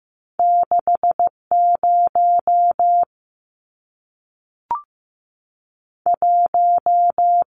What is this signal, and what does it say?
Morse code, heard as a keyed tone, about 15 wpm, 700 Hz. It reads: 60 1
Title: Numbers - 2-Digit 15wpm